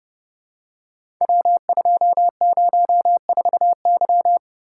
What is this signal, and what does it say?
Morse code, heard as a keyed tone, about 30 wpm, 700 Hz. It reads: W204Y
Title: Code Groups - 5 Letter:Numbers - 30wpm